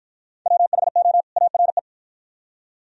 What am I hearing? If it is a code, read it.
Morse code, heard as a keyed tone, about 48 wpm, 700 Hz. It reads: WHO ARE